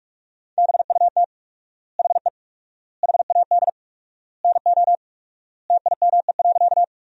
Morse code, heard as a keyed tone, about 42 wpm, 700 Hz. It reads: BUT HE HAD NO TIME.
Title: 4x Word-Spacing - Sentences from Top 100 Words - 45wpm